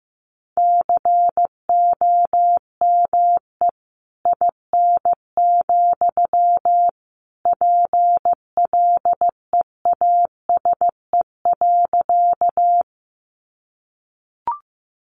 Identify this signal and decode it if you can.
Morse code, heard as a keyed tone, about 15 wpm, 700 Hz. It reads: COME IN, PLEASE.
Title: Sentences from Top 1500 Words 15wpm